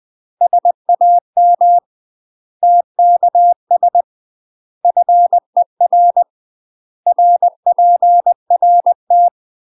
Morse code, heard as a keyed tone, about 20 wpm, 700 Hz. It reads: SAM TKS FER RPRT
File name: UK MASTER QSOs TRAINING/20WPM